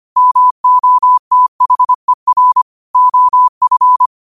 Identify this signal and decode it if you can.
Morse code, heard as a keyed tone, about 25 wpm, 1000 Hz. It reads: MOTHER OF
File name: qotdfiles / 25wpm